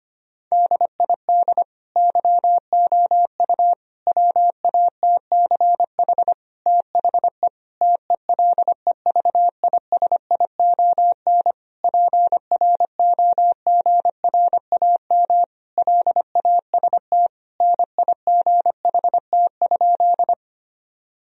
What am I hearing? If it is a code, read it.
Morse code, heard as a keyed tone, about 25 wpm, 700 Hz. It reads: DID YOU WATCH THE TELEVISION PROGRAM LAST NIGHT?
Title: Sentences from Top 1000 Words - Review Part 4 - 25wpm